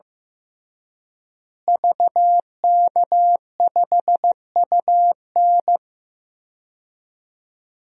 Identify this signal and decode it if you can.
Morse code, heard as a keyed tone, about 15 wpm, 700 Hz. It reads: VK5UN
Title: DXCC-100 Call Signs - 5 Characters - Repeated 3x - 15wpm